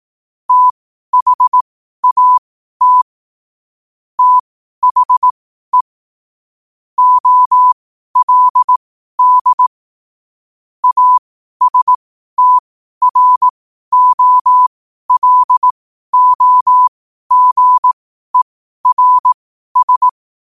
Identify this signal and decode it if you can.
Morse code, heard as a keyed tone, about 18 wpm, spaced as for 12 wpm, 1000 Hz. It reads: THAT THE OLD ASTROLOGERS
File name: qotdfiles / 12wpm